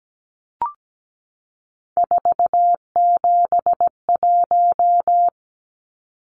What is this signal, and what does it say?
Morse code, heard as a keyed tone, about 17 wpm, 700 Hz. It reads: 471
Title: Numbers - 3-Digit 17wpm